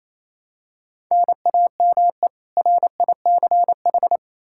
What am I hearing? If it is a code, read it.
Morse code, heard as a keyed tone, about 28 wpm, 700 Hz. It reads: NAME RICH
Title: K1USN SST QSO Practice 28wpm